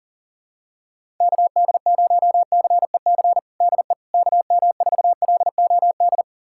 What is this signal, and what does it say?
Morse code, heard as a keyed tone, about 37 wpm, 700 Hz. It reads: KD0CEC DE KM4LOD
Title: USA Callsign DE Callsign - 6 Characters - 40wpm